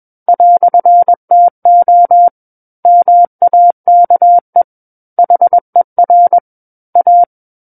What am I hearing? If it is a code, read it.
Morse code, heard as a keyed tone, about 21 wpm, 700 Hz. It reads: "TO MAKE HER A
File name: Book One / 12-Chapter 12 - 21 WPM